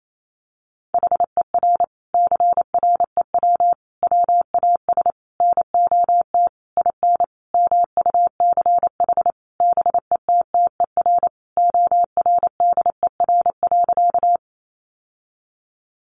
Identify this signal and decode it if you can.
Morse code, heard as a keyed tone, about 28 wpm, 700 Hz. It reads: HER CREW WAS NOT IN MUCH BETTER ORDER.
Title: Sentences from Top 1500 Words 28wpm